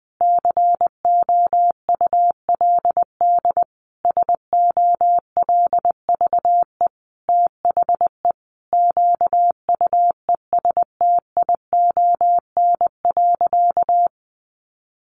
Morse code, heard as a keyed tone, about 20 wpm, 700 Hz. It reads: COULD SOLVE THE QUESTION.
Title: Sentences from Top 1000 Words 20wpm